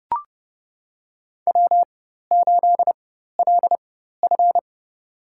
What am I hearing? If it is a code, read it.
Morse code, heard as a keyed tone, about 29 wpm, spaced as for 14 wpm, 700 Hz. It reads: W8LF
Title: USA Call Signs - 4 Characters - 4x Character Spacing - 30wpm